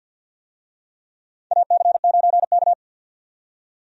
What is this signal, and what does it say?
Morse code, heard as a keyed tone, about 51 wpm, 700 Hz. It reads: AC9K
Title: USA Call Signs - 4 Characters - 50wpm